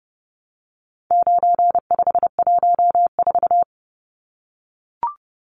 Morse code, heard as a keyed tone, about 30 wpm, 700 Hz. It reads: 9514
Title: Numbers - 4-Digit 30wpm